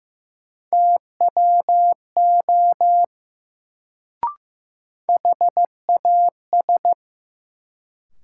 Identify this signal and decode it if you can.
Morse code, heard as a keyed tone, about 15 wpm, 700 Hz. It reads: TWO HAS
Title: Sentences from Top 100 Words - New Word Drill - 15wpm